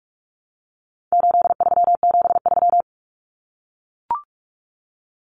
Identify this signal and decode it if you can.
Morse code, heard as a keyed tone, about 44 wpm, 700 Hz. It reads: 8373
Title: Numbers - 4-Digit 45wpm